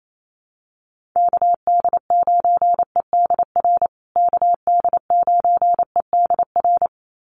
Morse code, heard as a keyed tone, about 28 wpm, 700 Hz. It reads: KD9EDR KD9EDR
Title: USA Call Signs Repeated 2x 28wpm